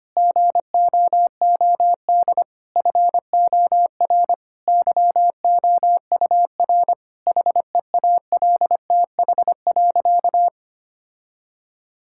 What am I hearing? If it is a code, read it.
Morse code, heard as a keyed tone, about 25 wpm, 700 Hz. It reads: GOOD FOR YOUR HEALTH.
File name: practice / 25wpm / Difficult Sentences with Hints - 25wpm